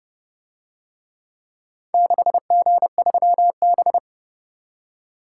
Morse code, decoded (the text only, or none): 6G3B